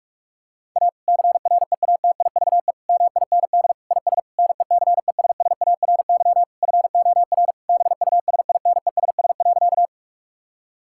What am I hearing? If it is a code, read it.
Morse code, heard as a keyed tone, about 46 wpm, 700 Hz. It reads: A CREATIVE MIND IS NECESSARY FOR BUSINESS.
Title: Sentences from Top 1000 Words - Review Part 5 - 45wpm